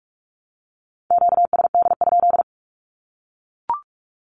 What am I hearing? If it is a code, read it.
Morse code, heard as a keyed tone, about 44 wpm, 700 Hz. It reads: QSD?
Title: Q-Codes 45wpm